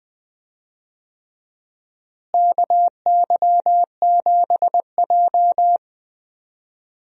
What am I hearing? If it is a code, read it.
Morse code, heard as a keyed tone, about 20 wpm, 700 Hz. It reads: KY7J
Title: USA Call Signs - 4 Characters - Repeated 3x - 20wpm